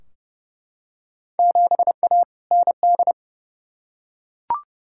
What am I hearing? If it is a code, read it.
Morse code, heard as a keyed tone, about 30 wpm, 700 Hz. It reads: 7A ND
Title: Field Day Exchanges Encore - 30wpm